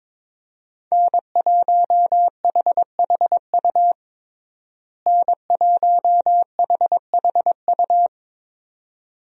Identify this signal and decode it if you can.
Morse code, heard as a keyed tone, about 21 wpm, 700 Hz. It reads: N1HHU N1HHU
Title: USA Call Signs - 5 Characters - Repeated 2x - 22wpm